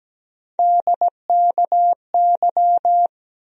Morse code, heard as a keyed tone, about 17 wpm, 700 Hz. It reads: DKY
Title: USA Call Sign Postfixes - 3 Letters 17wpm